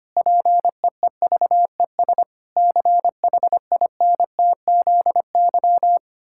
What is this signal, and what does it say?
Morse code, heard as a keyed tone, about 25 wpm, 700 Hz. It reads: PEEVES CHINTZY
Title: Pangram Challenge v2 - 25wpm